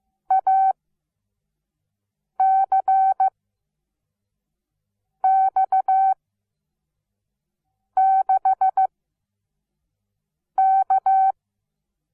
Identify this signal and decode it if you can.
Morse code, heard as a keyed tone, about 15 wpm, spaced as for 4 wpm, 770 Hz. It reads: ACX6K
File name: Onlinelehrgang / morsen / DJ4UF Morsen17